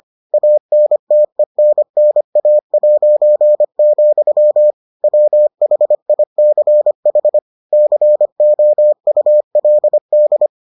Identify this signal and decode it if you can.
Morse code, heard as a keyed tone, about 25 wpm, 600 Hz. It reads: ANTENNA', WHICH COULD